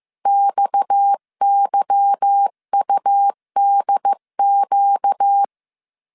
Morse code, heard as a keyed tone, about 15 wpm, 800 Hz. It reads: XYUDQ
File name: CW / DET / 1 9 1500C S B 14wpm 3302